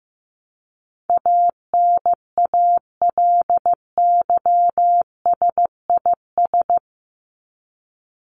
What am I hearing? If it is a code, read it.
Morse code, heard as a keyed tone, about 15 wpm, 700 Hz. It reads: ANALYSIS
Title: Top 501-750 Words 15wpm